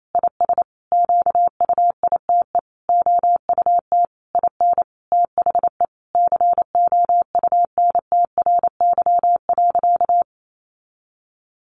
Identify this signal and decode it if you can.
Morse code, heard as a keyed tone, about 28 wpm, 700 Hz. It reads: IS QUITE OUT IN THE COUNTRY.
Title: Sentences from Top 700 Words 28wpm